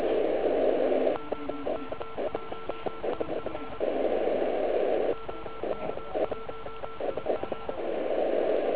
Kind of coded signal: Morse code